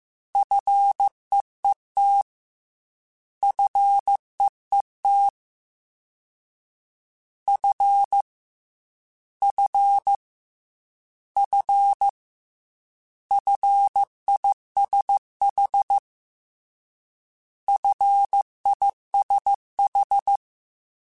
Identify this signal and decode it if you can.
Morse code, heard as a keyed tone, about 15 wpm, 800 Hz. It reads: FEET FEET F F F FISH FISH
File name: Fs / Single F mp3s / Single F Prefixes 3x2 Twice @13